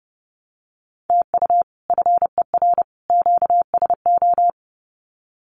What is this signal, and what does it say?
Morse code, heard as a keyed tone, about 30 wpm, 700 Hz. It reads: TU FER QSO
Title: Common QSO Phrases 30wpm